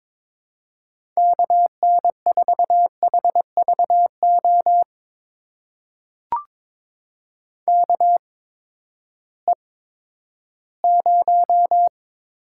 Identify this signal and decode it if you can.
Morse code, heard as a keyed tone, about 22 wpm, spaced as for 13 wpm, 700 Hz. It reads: KN4HVO KE0
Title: USA Call Signs - 6 Characters - 8x Character Spacing - 22wpm